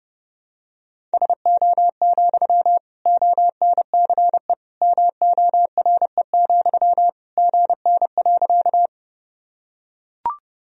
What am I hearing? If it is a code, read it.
Morse code, heard as a keyed tone, about 32 wpm, 700 Hz. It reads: SO, ONCE MORE, GN.
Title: Sentences from Top 300 Words and Abbreviations 30wpm